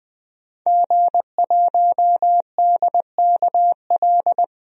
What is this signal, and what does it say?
Morse code, heard as a keyed tone, about 21 wpm, 700 Hz. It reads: G1DKL
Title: DXCC-100 Call Signs - 5 Characters - 20wpm